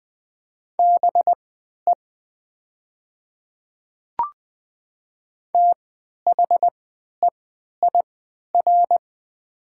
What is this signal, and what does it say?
Morse code, heard as a keyed tone, about 20 wpm, spaced as for 11 wpm, 700 Hz. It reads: BE THEIR
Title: 3x Character Spacing - Top 100 Words - 20wpm